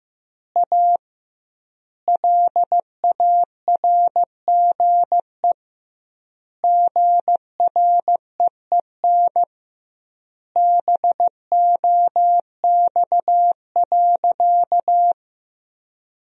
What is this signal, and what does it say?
Morse code, heard as a keyed tone, about 15 wpm, 700 Hz. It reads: A LARGE GREEN BOX.
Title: 2x Word-Spacing - Sentences from Top 500 Words - 15wpm